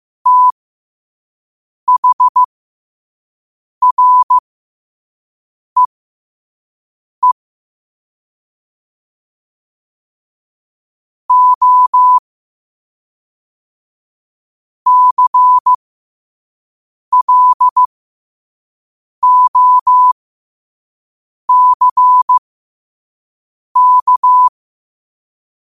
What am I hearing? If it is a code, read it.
Morse code, heard as a keyed tone, about 15 wpm, spaced as for 5 wpm, 1000 Hz. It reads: THREE O CLOCK
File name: qotdfiles / 5wpm